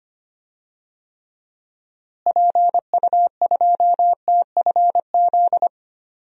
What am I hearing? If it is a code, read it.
Morse code, heard as a keyed tone, about 25 wpm, 700 Hz. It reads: PU2TFZ
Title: DXCC-100 Call Signs - 6 Characters - Repeated 2x - 25wpm